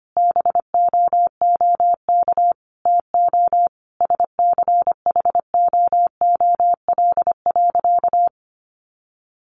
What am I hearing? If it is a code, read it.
Morse code, heard as a keyed tone, about 25 wpm, 700 Hz. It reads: BOOK TO SCHOOL.
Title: Sentences from Top 1400 Words 25wpm